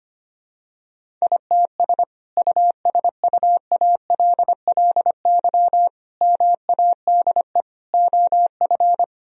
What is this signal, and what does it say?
Morse code, heard as a keyed tone, about 25 wpm, 700 Hz. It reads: ITS USUALLY MADE OF
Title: Sentences from Top 1000 Words - Review Part 3 - 25wpm